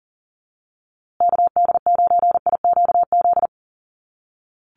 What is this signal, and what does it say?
Morse code, heard as a keyed tone, about 40 wpm, 700 Hz. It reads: KD9IQZ
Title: USA Call Signs Repeated 2x 40wpm